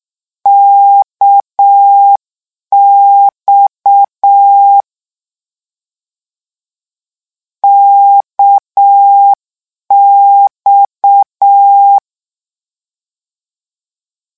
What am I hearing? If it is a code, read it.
Morse code, heard as a keyed tone, about 6 wpm, 800 Hz. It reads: KX KX
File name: morse mp3 / lecon 1-vitesse 6